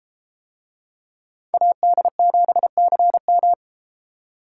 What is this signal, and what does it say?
Morse code, heard as a keyed tone, about 32 wpm, 700 Hz. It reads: AD7CM